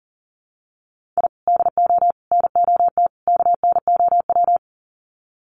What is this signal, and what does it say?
Morse code, heard as a keyed tone, about 40 wpm, 700 Hz. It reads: I DO NOT KNOW
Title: Sets of 4 Words 40wpm